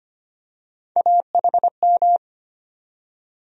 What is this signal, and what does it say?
Morse code, heard as a keyed tone, about 25 wpm, 700 Hz. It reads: AHM
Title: USA Call Sign Postfixes - 3 Letters 25wpm